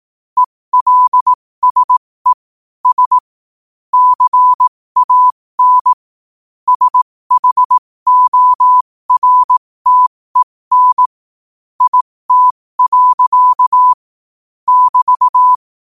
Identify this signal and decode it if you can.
Morse code, heard as a keyed tone, about 18 wpm, spaced as for 15 wpm, 1000 Hz. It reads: ELSE S CAN SHORTEN IT. <BT>
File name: 2024-11-18-15wpm